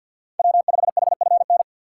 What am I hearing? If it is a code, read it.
Morse code, heard as a keyed tone, about 52 wpm, 700 Hz. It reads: W5HFN
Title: USA Call Signs - 2x Character Spacing - 50wpm